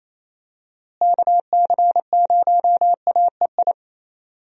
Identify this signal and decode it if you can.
Morse code, heard as a keyed tone, about 28 wpm, 700 Hz. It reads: KC0AEI
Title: USA Call Signs - 6 Characters - Repeated 2x - 28wpm